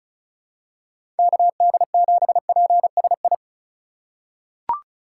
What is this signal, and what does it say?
Morse code, heard as a keyed tone, about 34 wpm, 700 Hz. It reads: KD7PSI
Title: USA Call Signs - 4x Character Spacing - 35wpm